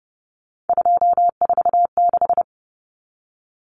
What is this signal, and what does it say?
Morse code, heard as a keyed tone, about 30 wpm, 700 Hz. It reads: 246
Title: Common Cut Numbers - 3-Digit 30wpm